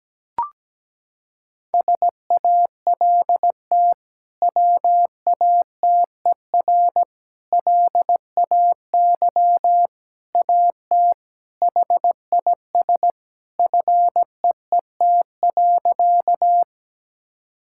Morse code, heard as a keyed tone, about 17 wpm, 700 Hz. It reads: SALT WATER LAY AT HIS FEET.
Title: Sentences from Top 1000 Words - Review Part 5 - 17wpm